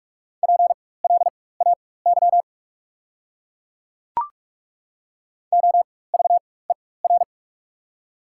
Morse code, heard as a keyed tone, about 43 wpm, spaced as for 21 wpm, 690 Hz. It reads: PLAY OVER
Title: 4x Character Spacing - Top 300 Words - 45wpm